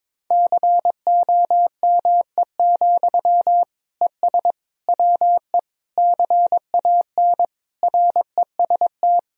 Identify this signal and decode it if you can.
Morse code, heard as a keyed tone, about 22 wpm, 700 Hz. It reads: COME, ES WE CAN REST
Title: Sentences from Top 500 Words and Abbreviations 22wpm